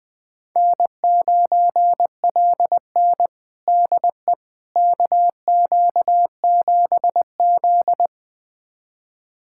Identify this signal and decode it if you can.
Morse code, heard as a keyed tone, about 20 wpm, 700 Hz. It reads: N9LN DE KQ7Z
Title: USA Callsign DE Callsign - 4 Characters - 20wpm